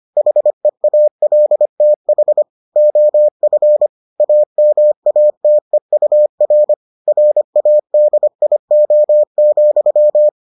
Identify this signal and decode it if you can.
Morse code, heard as a keyed tone, about 25 wpm, 600 Hz. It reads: HEALTH OF AMATEUR RADIO,